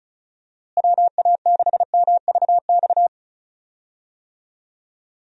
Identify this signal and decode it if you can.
Morse code, heard as a keyed tone, about 36 wpm, 700 Hz. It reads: WA6MVX
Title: USA Call Signs - 6 Characters - Repeated 2x - 35wpm